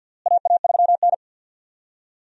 Morse code, heard as a keyed tone, about 50 wpm, 700 Hz. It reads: AA3N
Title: USA Call Signs - 4 Characters - Repeated 2x - 50wpm